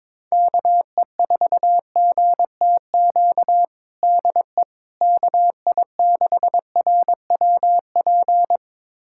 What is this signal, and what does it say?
Morse code, heard as a keyed tone, about 21 wpm, 700 Hz. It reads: KE4GTQ DE KI6RWP
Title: USA Callsign DE Callsign - 6 Characters - 22wpm